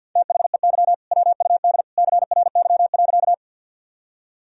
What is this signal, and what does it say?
Morse code, heard as a keyed tone, about 52 wpm, 700 Hz. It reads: THEY WUD CRY.
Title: Sentences from Top 500 Words and Abbreviations 50wpm